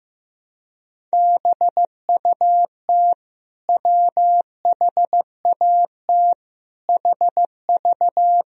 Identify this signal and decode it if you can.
Morse code, heard as a keyed tone, about 15 wpm, 700 Hz. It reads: BUT WHAT HV
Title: Sentences from Top 100 Words and Abbreviations 15wpm